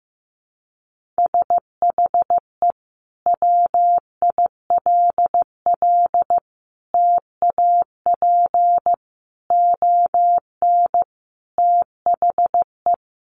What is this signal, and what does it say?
Morse code, heard as a keyed tone, about 14 wpm, 700 Hz. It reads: SHE WILL TAP ON THE
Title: Sentences from Top 1400 Words 15wpm